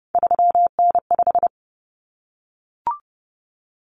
Morse code, heard as a keyed tone, about 30 wpm, 700 Hz. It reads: 3N5
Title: Common Cut Numbers - 3-Digit 30wpm